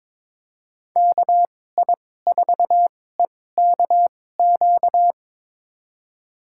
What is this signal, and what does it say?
Morse code, heard as a keyed tone, about 22 wpm, spaced as for 16 wpm, 700 Hz. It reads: KI4EKQ
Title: USA Call Signs - 6 Characters - 2x Character Spacing - 22wpm